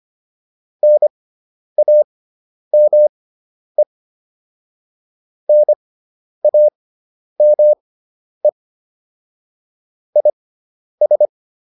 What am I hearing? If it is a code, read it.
Morse code, heard as a keyed tone, about 25 wpm, spaced as for 10 wpm, 600 Hz. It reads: NAME NAME IS
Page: intermediate-practice-files